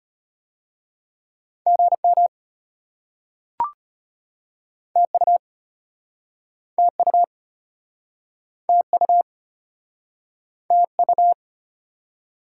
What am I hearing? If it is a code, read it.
Morse code, heard as a keyed tone, about 37 wpm, 700 Hz. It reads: GM TU TU TU TU